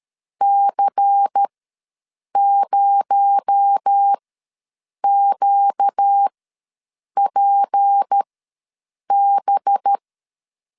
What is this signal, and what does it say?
Morse code, heard as a keyed tone, about 13 wpm, spaced as for 7 wpm, 800 Hz. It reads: C0QPB
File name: LEZIONI / 6 PKX0 / 6 PKX0 ws 12wpm 3448